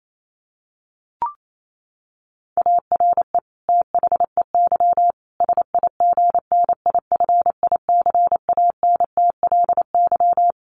Morse code, heard as a keyed tone, about 28 wpm, 700 Hz. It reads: ARE THEY SIGNIFICANTLY
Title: Sentences from Top 1500 Words 28wpm